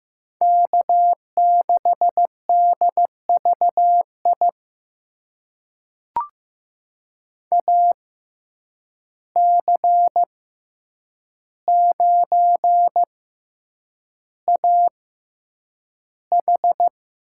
Morse code, heard as a keyed tone, about 15 wpm, spaced as for 8 wpm, 700 Hz. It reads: K6DVI AC9AH
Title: USA Call Signs - 5 Characters - 6x Character Spacing - 15wpm